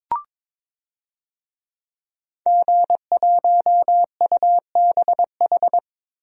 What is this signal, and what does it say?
Morse code, heard as a keyed tone, about 22 wpm, 700 Hz. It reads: G1UBH
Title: DXCC-100 Call Signs - 5 Characters - Repeated 3x - 22wpm